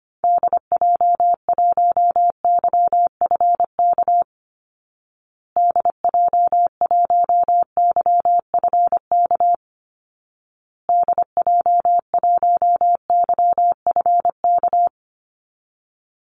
Morse code, heard as a keyed tone, about 25 wpm, 700 Hz. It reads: DJ1YFK DJ1YFK DJ1YFK
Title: Call Signs Repeated 2x - 25wpm